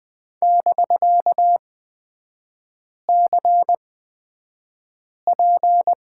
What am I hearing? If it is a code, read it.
Morse code, heard as a keyed tone, about 20 wpm, spaced as for 5 wpm, 700 Hz. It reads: <BK>CP
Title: 250 Lesson 40e - BK - Three Characters 20wpm